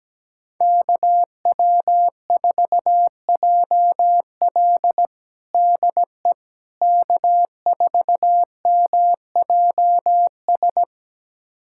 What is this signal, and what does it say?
Morse code, heard as a keyed tone, about 17 wpm, 700 Hz. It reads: KW4JL DE K4MJS
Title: USA Callsign DE Callsign - 5 Characters - 17wpm